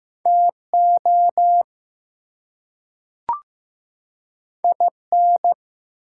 Morse code, heard as a keyed tone, about 15 wpm, 700 Hz. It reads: TO IN
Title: Two Letter Words - 15wpm